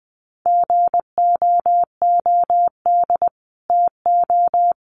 Morse code, heard as a keyed tone, about 20 wpm, 700 Hz. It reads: GOOD TO
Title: Sets of 5 Words Encore2 20wpm